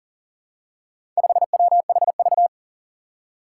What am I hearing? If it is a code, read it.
Morse code, heard as a keyed tone, about 40 wpm, 700 Hz. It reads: 5WHV
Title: Four Challenging Letter-Number Combinations 40wpm